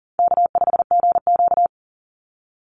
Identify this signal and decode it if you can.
Morse code, heard as a keyed tone, about 40 wpm, 700 Hz. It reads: K5GQ
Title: Call Signs - 40wpm